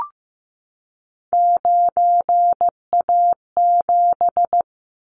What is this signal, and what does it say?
Morse code, heard as a keyed tone, about 15 wpm, 700 Hz. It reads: 9A7
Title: Callsign Prefixes for Top 100 DXCC - 15wpm